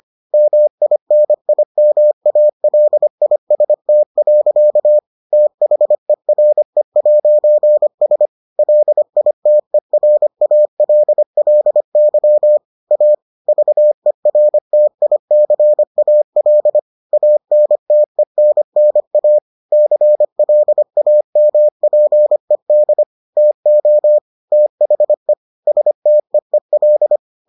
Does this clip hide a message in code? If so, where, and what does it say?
Yes: Morse code from 0.3 to 27.2 s, heard as a keyed tone, about 25 wpm, 600 Hz. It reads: MINIMALIST. THERE'S LITERALLY A VERTICAL ANTENNA CLAMPED TO THE STEEL